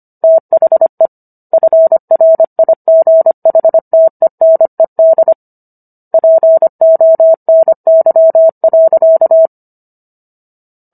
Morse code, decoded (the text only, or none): THE FRIGHTENED PONY.